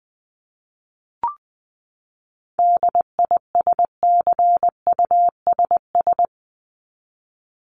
Morse code, heard as a keyed tone, about 20 wpm, 700 Hz. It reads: DISCUSS
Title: Top 751-1000 Words 20wpm